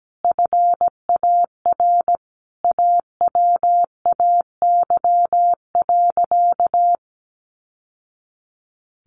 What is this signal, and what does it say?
Morse code, heard as a keyed tone, about 17 wpm, 700 Hz. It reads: FAR AWAY.
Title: Sentences from Top 1100 Words 17wpm